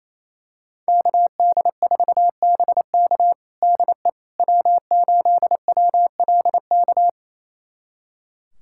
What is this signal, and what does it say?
Morse code, heard as a keyed tone, about 28 wpm, 700 Hz. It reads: KD4BK DE W8WLK
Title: USA Callsign DE Callsign 28wpm